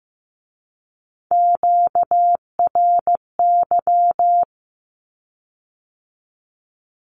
Morse code, heard as a keyed tone, about 15 wpm, 700 Hz. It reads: QRY
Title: Q-Codes with Spoken Meaning - 15wpm